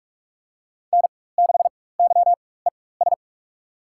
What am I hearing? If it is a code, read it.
Morse code, heard as a keyed tone, about 42 wpm, spaced as for 21 wpm, 700 Hz. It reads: N6YES
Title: USA Call Signs - 5 Characters - 4x Character Spacing - 45wpm